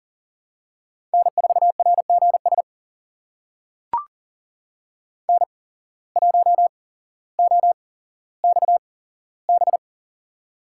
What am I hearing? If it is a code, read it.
Morse code, heard as a keyed tone, about 40 wpm, spaced as for 17 wpm, 700 Hz. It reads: N4RGS N1OXB
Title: USA Call Signs - 5 Characters - 8x Character Spacing - 40wpm